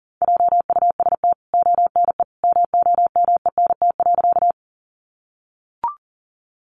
Morse code, heard as a keyed tone, about 40 wpm, 700 Hz. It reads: JUST ONE MOMENT.
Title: Sentences from Top 1000 Words 40wpm